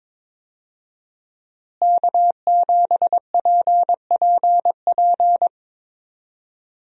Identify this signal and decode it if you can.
Morse code, heard as a keyed tone, about 22 wpm, 700 Hz. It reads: K7PPP
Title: USA Call Signs - 5 Characters - Repeated 2x - 22wpm